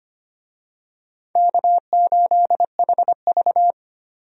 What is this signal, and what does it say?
Morse code, heard as a keyed tone, about 25 wpm, 700 Hz. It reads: K8HV
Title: USA Call Signs - 4 Characters - 25wpm